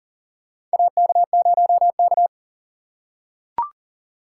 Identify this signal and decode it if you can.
Morse code, heard as a keyed tone, about 40 wpm, 700 Hz. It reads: AK0K
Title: USA Call Signs - 4 Characters - 2x Character Spacing - 40wpm